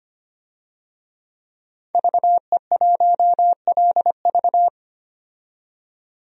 Morse code, heard as a keyed tone, about 25 wpm, 700 Hz. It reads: VE1LV
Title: DXCC-100 Call Signs - 5 Characters - 25wpm